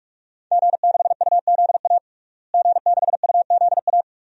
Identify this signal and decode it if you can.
Morse code, heard as a keyed tone, about 46 wpm, 700 Hz. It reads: G6UZA G6UZA
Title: DXCC-100 Call Signs - 5 Characters - Repeated 2x - 45wpm